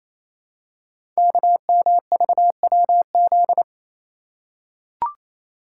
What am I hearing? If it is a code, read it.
Morse code, heard as a keyed tone, about 27 wpm, 700 Hz. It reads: KMVWZ
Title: Code Groups - 5 Letters - 28wpm